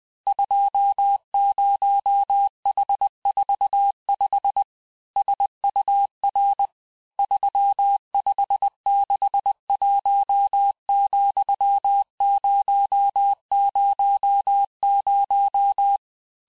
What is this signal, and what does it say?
Morse code, heard as a keyed tone, about 20 wpm, 800 Hz. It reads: 20H45 SUR 3561,000
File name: cours-du-lundi files / cours-1584-teste-chiffre-vitesse-19mpm